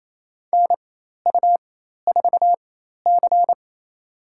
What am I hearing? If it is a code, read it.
Morse code, heard as a keyed tone, about 28 wpm, spaced as for 13 wpm, 700 Hz. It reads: NU4C
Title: USA Call Signs - 4 Characters - 4x Character Spacing - 28wpm